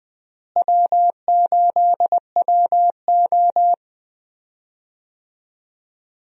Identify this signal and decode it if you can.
Morse code, heard as a keyed tone, about 21 wpm, 700 Hz. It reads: W8WO
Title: USA Call Signs - 4 Characters - Repeated 2x - 20wpm